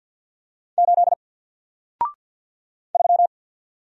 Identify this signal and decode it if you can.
Morse code, heard as a keyed tone, about 50 wpm, 700 Hz. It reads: 8 3
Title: Single Character (Letter:Number:Punctuation:Prosign) 50wpm